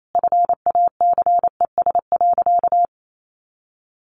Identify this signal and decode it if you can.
Morse code, heard as a keyed tone, about 28 wpm, 700 Hz. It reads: FACES.
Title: A Sound Of Thunder 28wpm